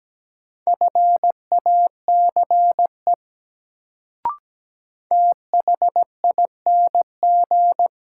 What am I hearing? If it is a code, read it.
Morse code, heard as a keyed tone, about 17 wpm, 700 Hz. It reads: FACE THING
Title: Sets of 3 Words - New Word Drill - 17wpm